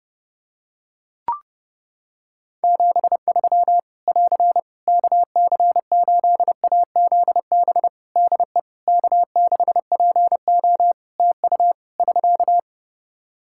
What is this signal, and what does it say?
Morse code, heard as a keyed tone, about 30 wpm, 700 Hz. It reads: 73 <AR> KC8AZB DE K6PO TU <SK>
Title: Practice QSO - Exchange 4 - Ending - 30wpm